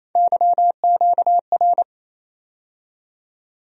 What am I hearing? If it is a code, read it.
Morse code, heard as a keyed tone, about 28 wpm, 700 Hz. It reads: YQR
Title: USA Call Sign Postfixes - 3 Letters 28wpm